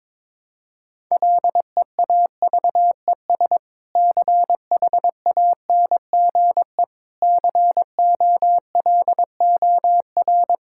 Morse code, heard as a keyed tone, about 22 wpm, 700 Hz. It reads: LEAVES CHANGE COLOR
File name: Difficult Sentences with Hints - 22wpm